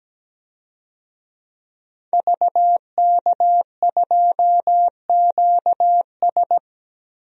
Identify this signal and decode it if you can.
Morse code, heard as a keyed tone, about 17 wpm, 700 Hz. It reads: VK2QS
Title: DXCC-100 Call Signs - 5 Characters - Repeated 2x - 17wpm